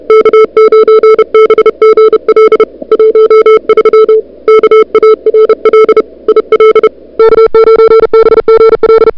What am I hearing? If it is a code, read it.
Morse code, heard as a keyed tone, about 34 wpm, 440 Hz. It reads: K9BGL 13 KARL IL K9BGL
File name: n6tr / Fab2007CWSprintHour1